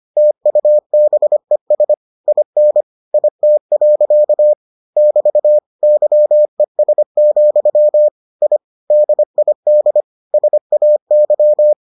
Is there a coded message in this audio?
Yes: Morse code from 0.2 to 11.7 s, heard as a keyed tone, about 25 wpm, 600 Hz. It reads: TUBES IN IT. <BT> YES, I DID SAY